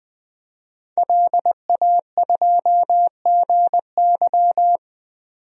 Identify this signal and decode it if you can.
Morse code, heard as a keyed tone, about 20 wpm, 700 Hz. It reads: LA2GY
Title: DXCC-100 Call Signs - 5 Characters - 20wpm